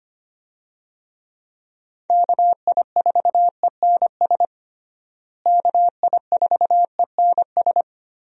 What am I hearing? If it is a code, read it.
Morse code, heard as a keyed tone, about 25 wpm, 700 Hz. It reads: KI4ENS KI4ENS
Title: USA Call Signs - 6 Characters - Repeated 3x - 25wpm